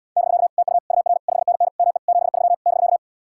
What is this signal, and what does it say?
Morse code, heard as a keyed tone, about 77 wpm, 700 Hz. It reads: <BK> TU UR 5NN ME 73 <BK>